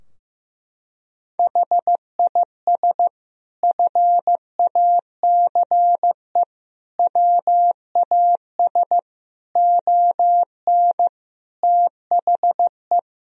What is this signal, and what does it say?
Morse code, heard as a keyed tone, about 15 wpm, 700 Hz. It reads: HIS FACE WAS ON THE
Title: Sentences from Top 1000 Words - Review Part 2 - 15wpm